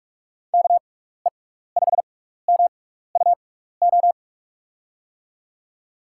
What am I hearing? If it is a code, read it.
Morse code, heard as a keyed tone, about 41 wpm, spaced as for 15 wpm, 700 Hz. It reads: KE5MUO
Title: USA Call Signs - 6 Characters - 6x Character Spacing - 45wpm